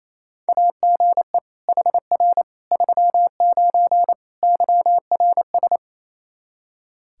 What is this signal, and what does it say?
Morse code, heard as a keyed tone, about 28 wpm, 700 Hz. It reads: AGE HR 39 YRS
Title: Practice QSO - Exchange 3 - More Info - 28wpm